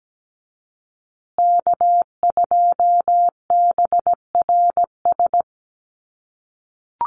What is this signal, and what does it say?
Morse code, heard as a keyed tone, about 17 wpm, 700 Hz. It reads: K2BRS
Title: USA Call Signs Repeated 2x 17wpm